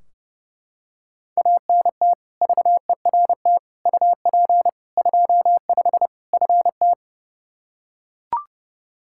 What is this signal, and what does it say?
Morse code, heard as a keyed tone, about 30 wpm, 700 Hz. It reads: ANT VERT UP 25 FT
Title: Practice QSO - Exchange 2 - Info - 30wpm